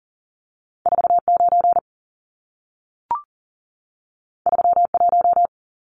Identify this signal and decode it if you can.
Morse code, heard as a keyed tone, about 40 wpm, 700 Hz. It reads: 49 31
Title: Numbers - 2-Digit 40wpm